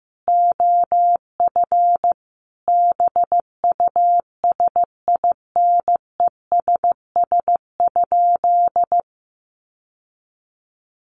Morse code, heard as a keyed tone, about 14 wpm, 700 Hz. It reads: OF BUSINESS?
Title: Sentences from Top 1300 Words 15wpm